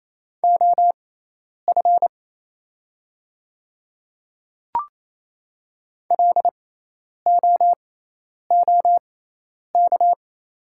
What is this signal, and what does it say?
Morse code, heard as a keyed tone, about 28 wpm, spaced as for 10 wpm, 700 Hz. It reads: OF LOOK
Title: 6x Character Spacing - Top 100 Words - 28wpm